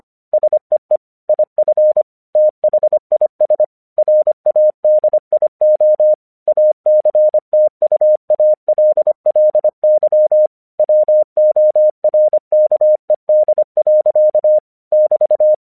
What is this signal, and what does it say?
Morse code, heard as a keyed tone, about 25 wpm, 610 Hz. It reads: SEE IF THIS RADIO ACTUALLY WORKED. <BT>
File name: 1 English 25wpm / AQC26